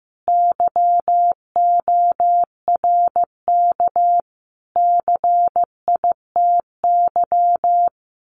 Morse code, heard as a keyed tone, about 14 wpm, 700 Hz. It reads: YORK CITY
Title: Sets of 5 Words Encore2 15wpm